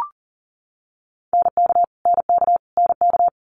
Morse code, heard as a keyed tone, about 40 wpm, 700 Hz. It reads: NK NK NK
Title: Multi-letter Phonemes Repeated 3x 40wpm